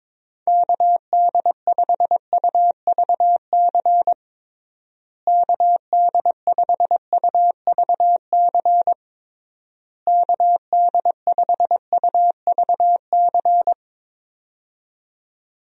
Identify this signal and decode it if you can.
Morse code, heard as a keyed tone, about 22 wpm, 700 Hz. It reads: KD5UVC KD5UVC KD5UVC
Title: USA Call Signs - 6 Characters - Repeated 3x - 22wpm